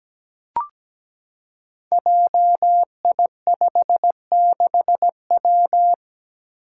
Morse code, heard as a keyed tone, about 17 wpm, 700 Hz. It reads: JI56W
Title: Five Challenging Letter-Number Combinations 17wpm